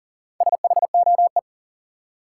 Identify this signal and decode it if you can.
Morse code, heard as a keyed tone, about 40 wpm, 700 Hz. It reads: SHOE
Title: Four Letter Words - 40wpm